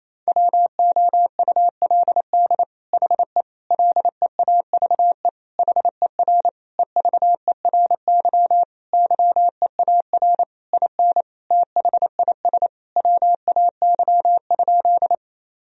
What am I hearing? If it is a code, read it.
Morse code, heard as a keyed tone, about 28 wpm, 700 Hz. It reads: WOULD HE LEAVE HER EVERY YEAR IN THIS WAY?
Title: Sentences from Top 1000 Words - Review Part 1 - 28wpm